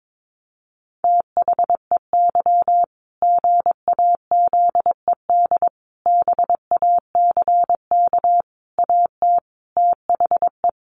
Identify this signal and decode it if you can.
Morse code, heard as a keyed tone, about 22 wpm, 700 Hz. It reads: THEY GAZED BACK AT THE
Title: A Sound Of Thunder 22wpm